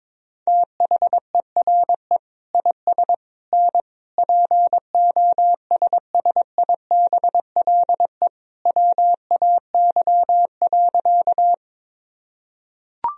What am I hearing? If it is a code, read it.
Morse code, heard as a keyed tone, about 22 wpm, 700 Hz. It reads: THERE IS N POSSIBLE WAY.
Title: Sentences from Top 500 Words and Abbreviations 22wpm